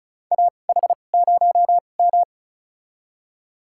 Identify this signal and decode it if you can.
Morse code, heard as a keyed tone, about 36 wpm, spaced as for 25 wpm, 700 Hz. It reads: AH0M
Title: USA Call Signs - 4 Characters - 2x Character Spacing - 35wpm